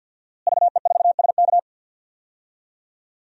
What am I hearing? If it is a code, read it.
Morse code, heard as a keyed tone, about 51 wpm, 700 Hz. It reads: VE4SK
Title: DXCC-100 Call Signs - 5 Characters - Repeated 2x - 50wpm